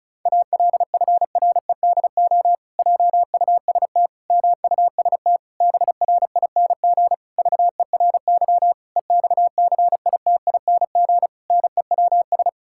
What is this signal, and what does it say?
Morse code, heard as a keyed tone, about 36 wpm, 700 Hz. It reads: ALFREDO JUST MUST BRING VERY EXCITING NEWS
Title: Pangram Challenge v2 - 35wpm